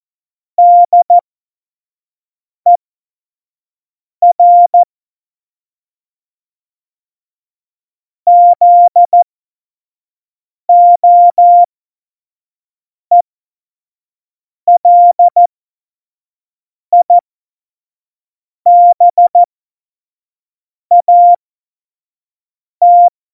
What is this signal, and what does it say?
Morse code, heard as a keyed tone, about 14 wpm, spaced as for 5 wpm, 700 Hz. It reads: DER ZOELIBAT
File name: Morsekurs / Witze / Witz 19. Januar 2021, 25 F70